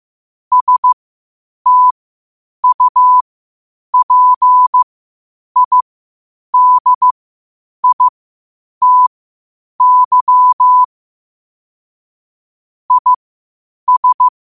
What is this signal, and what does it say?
Morse code, heard as a keyed tone, about 15 wpm, spaced as for 8 wpm, 1000 Hz. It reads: STUPIDITY IS
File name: qotdfiles / 8wpm